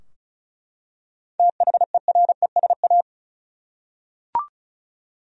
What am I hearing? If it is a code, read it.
Morse code, heard as a keyed tone, about 34 wpm, 700 Hz. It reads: THERESA
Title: USA Common Names - 35wpm